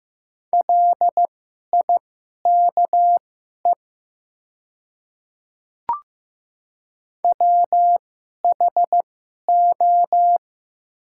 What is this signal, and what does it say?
Morse code, heard as a keyed tone, about 15 wpm, spaced as for 11 wpm, 700 Hz. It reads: LIKE WHO
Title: 2x Character Spacing - Top 100 Words - 15wpm